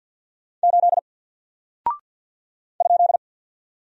Morse code, heard as a keyed tone, about 48 wpm, 700 Hz. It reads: 8 ?
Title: Single Character (Letter:Number:Punctuation:Prosign) Alt 50wpm